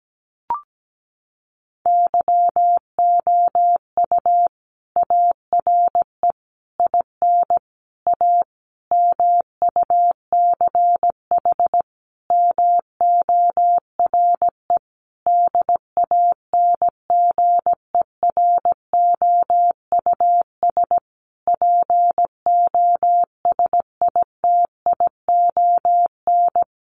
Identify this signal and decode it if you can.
Morse code, heard as a keyed tone, about 17 wpm, 700 Hz. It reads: YOU ARE IN A MUCH MORE DANGEROUS POSITION
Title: Sentences from Top 1400 Words 17wpm